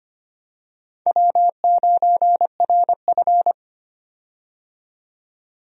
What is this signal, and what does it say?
Morse code, heard as a keyed tone, about 25 wpm, 700 Hz. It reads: W9RF
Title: USA Call Signs - 4 Characters - 25wpm